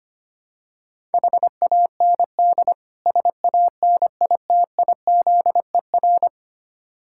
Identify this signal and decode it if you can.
Morse code, heard as a keyed tone, about 25 wpm, 700 Hz. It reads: HAND SANITIZER
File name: practice / 25wpm / Difficult Words with Hints - 25wpm